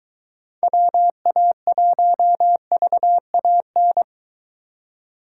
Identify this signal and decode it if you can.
Morse code, heard as a keyed tone, about 22 wpm, 700 Hz. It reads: WA1VAN